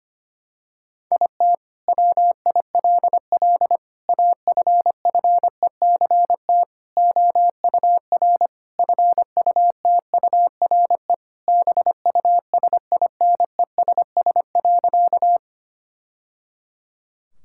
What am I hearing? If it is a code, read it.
Morse code, heard as a keyed tone, about 25 wpm, 700 Hz. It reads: IT WILL AFFECT OUR FUTURE BUSINESS.
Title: Sentences from Top 1000 Words - Review Part 6 - 25wpm